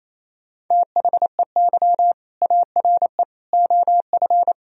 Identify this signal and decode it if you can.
Morse code, heard as a keyed tone, about 28 wpm, 700 Hz. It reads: THEY ARE OF
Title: Sentences from Top 1000 Words - Review Part 4 - 28wpm